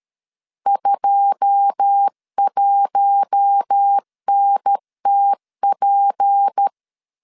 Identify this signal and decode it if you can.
Morse code, heard as a keyed tone, about 13 wpm, 800 Hz. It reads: 21NTP